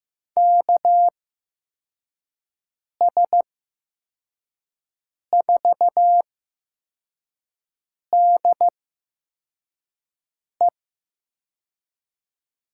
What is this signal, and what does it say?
Morse code, heard as a keyed tone, about 15 wpm, spaced as for 4 wpm, 700 Hz. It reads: KS4DE
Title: USA Call Signs - 8x Character Spacing - 15wpm